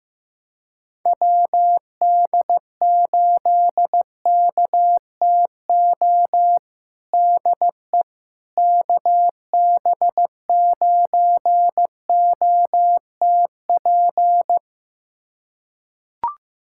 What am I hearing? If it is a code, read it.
Morse code, heard as a keyed tone, about 15 wpm, 700 Hz. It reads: WD8KTO DE KB9OTP
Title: USA Callsign DE Callsign - 6 Characters - 15wpm